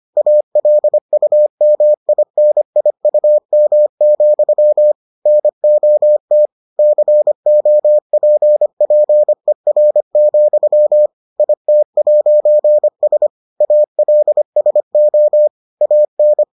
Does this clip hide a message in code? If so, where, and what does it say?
Yes: Morse code from 0.2 to 16.4 s, heard as a keyed tone, about 25 wpm, 600 Hz. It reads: ALUMINIUM, NOT COPPER, IT'S ALSO AN